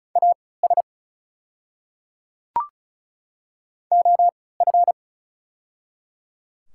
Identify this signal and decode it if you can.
Morse code, heard as a keyed tone, about 33 wpm, spaced as for 19 wpm, 690 Hz. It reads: AS OF
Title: 3x Character Spacing - Top 100 Words - 35wpm